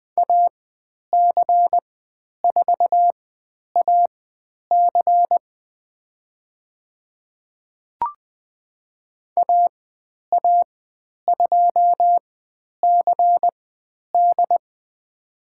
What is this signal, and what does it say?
Morse code, heard as a keyed tone, about 20 wpm, spaced as for 10 wpm, 700 Hz. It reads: AC4AC AA2CD
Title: 109 Lesson 18g - Letter m - Call Signs 20wpm